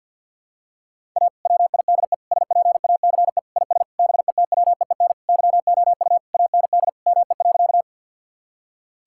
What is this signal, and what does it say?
Morse code, heard as a keyed tone, about 52 wpm, 700 Hz. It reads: A WIDE SPACE IS BETWEEN YOU AND ME.
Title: Sentences from Top 1000 Words - Review Part 6 - 50wpm